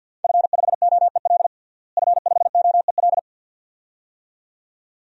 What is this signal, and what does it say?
Morse code, heard as a keyed tone, about 50 wpm, 700 Hz. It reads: F5OEL F5OEL
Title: DXCC-100 Call Signs - 5 Characters - Repeated 2x - 50wpm